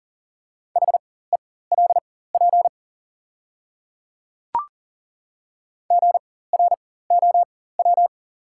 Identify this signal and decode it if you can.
Morse code, heard as a keyed tone, about 40 wpm, spaced as for 19 wpm, 700 Hz. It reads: HELP GROW
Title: 4x Character Spacing - Top 300 Words - 40wpm